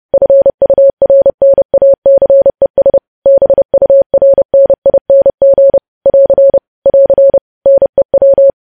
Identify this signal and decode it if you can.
Morse code, heard as a keyed tone, about 30 wpm, 550 Hz. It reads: FURNACES BURNING <AR> <AR> NEW
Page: news headlines in morse code at 30 wpm